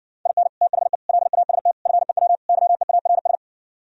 Morse code, heard as a keyed tone, about 68 wpm, 700 Hz. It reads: IN THE LAST FEW YEARS